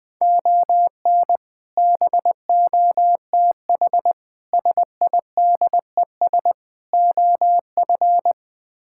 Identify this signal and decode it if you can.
Morse code, heard as a keyed tone, about 20 wpm, 700 Hz. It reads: ON BOTH SIDES OF
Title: Build A Phrase - 4-Word Phrases - 20wpm